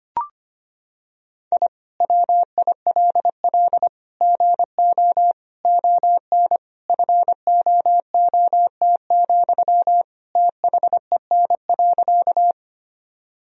Morse code, heard as a keyed tone, about 25 wpm, 700 Hz. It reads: I WILL GO ON FOOT, THEN.
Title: Sentences from Top 1000 Words - Review Part 3 - 25wpm